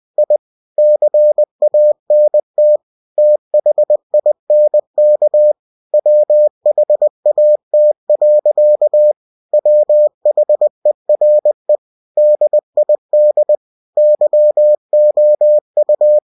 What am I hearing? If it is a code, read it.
Morse code, heard as a keyed tone, about 20 wpm, 600 Hz. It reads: I CANT THINK WHAT. WHERE DID YOU
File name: cw-books / winnie-the-pooh / winnie-20wpm-page-0032